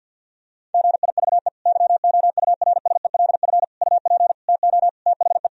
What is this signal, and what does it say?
Morse code, heard as a keyed tone, about 51 wpm, 700 Hz. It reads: GIVE YOURSELF UP TO THE
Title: Sentences from Top 1000 Words - Review Part 4 - 50wpm